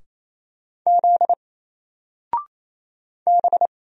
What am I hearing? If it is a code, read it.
Morse code, heard as a keyed tone, about 27 wpm, 700 Hz. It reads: Z B
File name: Single-Letter-Number-28wpm